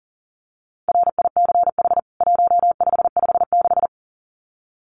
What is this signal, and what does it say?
Morse code, heard as a keyed tone, about 40 wpm, 700 Hz. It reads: RICH 1556
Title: CWT Exchange 40wpm